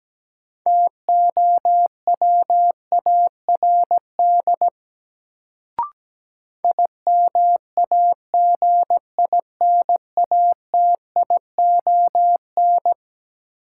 Morse code, heard as a keyed tone, about 17 wpm, 700 Hz. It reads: TOWARD IMAGINATION
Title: Sentences from Top 1300 Words - New Word Drill Alt - 17wpm